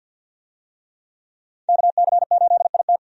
Morse code, heard as a keyed tone, about 50 wpm, 700 Hz. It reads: KC8IT
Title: USA Call Signs - 5 Characters - 50wpm